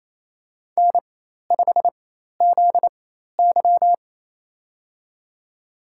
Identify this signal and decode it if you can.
Morse code, heard as a keyed tone, about 28 wpm, spaced as for 13 wpm, 700 Hz. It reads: N5ZY
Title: USA Call Signs - 4 Characters - 4x Character Spacing - 28wpm